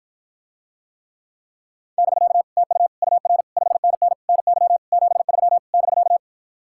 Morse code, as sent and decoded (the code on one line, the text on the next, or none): -...-.- / - ..- / ..- .-. / ..... -. -. / -. -.-- / --... ...-- / -...-.-
<BK> TU UR 5NN NY 73 <BK>